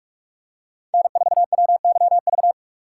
Morse code, heard as a keyed tone, about 44 wpm, 700 Hz. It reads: N4WYV